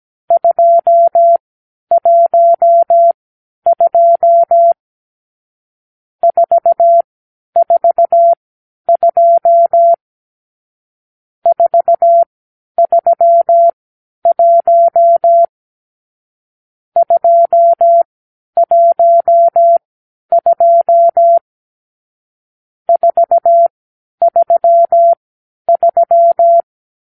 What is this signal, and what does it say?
Morse code, heard as a keyed tone, about 17 wpm, spaced as for 10 wpm, 680 Hz. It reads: 212 442 431 212 433
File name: k7qo / Lesson 46 (1-4)